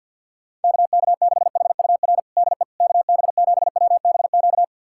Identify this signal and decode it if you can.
Morse code, heard as a keyed tone, about 50 wpm, 700 Hz. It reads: KK6HUR DE KB7WBQ
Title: USA Callsign DE Callsign - 6 Characters - 50wpm